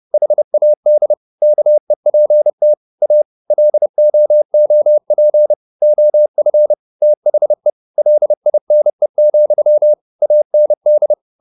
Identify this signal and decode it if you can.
Morse code, heard as a keyed tone, about 30 wpm, 600 Hz. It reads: HAD KEPT A LOOP OF THE LINE, AND
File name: the old man and the sea / the old man and the sea 1 30wpm